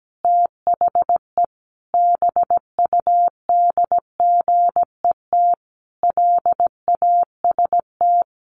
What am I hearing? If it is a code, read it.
Morse code, heard as a keyed tone, about 17 wpm, 700 Hz. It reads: THE BUDGET LAST
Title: Sentences from Top 1200 Words 17wpm